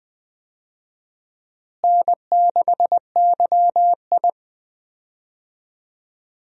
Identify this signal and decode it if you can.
Morse code, heard as a keyed tone, about 20 wpm, 700 Hz. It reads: N6YI
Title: USA Call Signs - 4 Characters - Repeated 2x - 20wpm